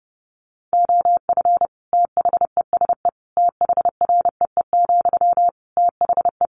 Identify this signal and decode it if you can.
Morse code, heard as a keyed tone, about 30 wpm, 700 Hz. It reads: OF THESE THREE, THE
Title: Sentences from Top 700 Words 30wpm